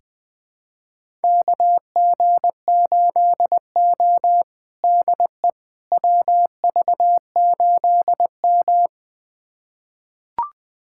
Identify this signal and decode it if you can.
Morse code, heard as a keyed tone, about 20 wpm, 700 Hz. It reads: KG8O DE WV8M
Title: USA Callsign DE Callsign - 4 Characters - 20wpm